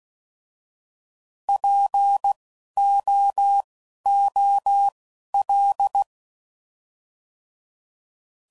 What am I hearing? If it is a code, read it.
Morse code, heard as a keyed tone, about 16 wpm, spaced as for 11 wpm, 800 Hz. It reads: POOL
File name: Prefixes / By First Letter / Ps